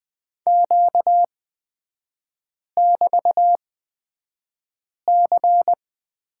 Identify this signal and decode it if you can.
Morse code, heard as a keyed tone, about 20 wpm, spaced as for 5 wpm, 700 Hz. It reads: Q<BT>C
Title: 236 Lesson 38e - Letter q - Three Characters 20wpm